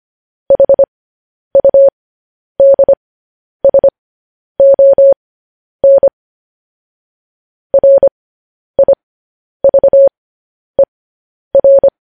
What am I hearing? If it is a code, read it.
Morse code, heard as a keyed tone, about 25 wpm, spaced as for 10 wpm, 550 Hz. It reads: HUDSON RIVER
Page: news headlines in morse code at 10 wpm